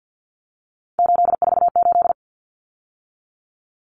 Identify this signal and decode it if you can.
Morse code, heard as a keyed tone, about 48 wpm, 700 Hz. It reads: 848
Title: Numbers - 3-Digit 50wpm